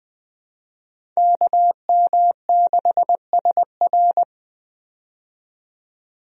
Morse code, heard as a keyed tone, about 20 wpm, 700 Hz. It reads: KM6SR
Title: USA Call Signs - 5 Characters - 20wpm